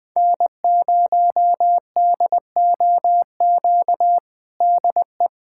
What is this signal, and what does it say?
Morse code, heard as a keyed tone, about 20 wpm, 700 Hz. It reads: N0DOQ DE
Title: USA Callsign DE Callsign 20wpm